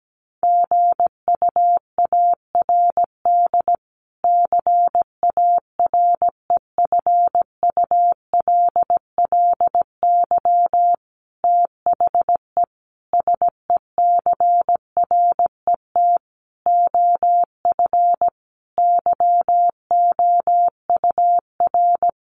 Morse code, heard as a keyed tone, about 17 wpm, 700 Hz. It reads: GUARD CAREFULLY THE SECRET OF YOUR
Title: Sentences from Top 1300 Words 17wpm